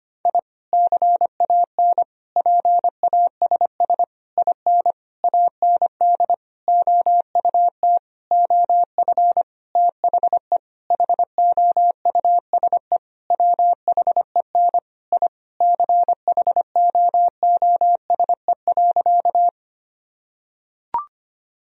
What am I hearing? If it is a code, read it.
Morse code, heard as a keyed tone, about 25 wpm, 700 Hz. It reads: I CAN PASS IN AND OUT OF THE HOUSE WHEN I CHOOSE.
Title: Sentences from Top 1000 Words - Review Part 4 - 25wpm